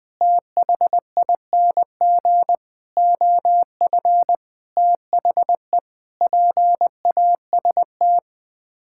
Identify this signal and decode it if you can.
Morse code, heard as a keyed tone, about 20 wpm, 700 Hz. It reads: THING OF THE PAST
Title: Sets of 6 Words - 20wpm